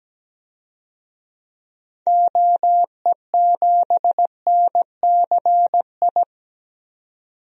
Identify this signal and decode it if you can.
Morse code, heard as a keyed tone, about 17 wpm, 700 Hz. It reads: OE7NCI
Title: DXCC-100 Call Signs - 6 Characters - 17wpm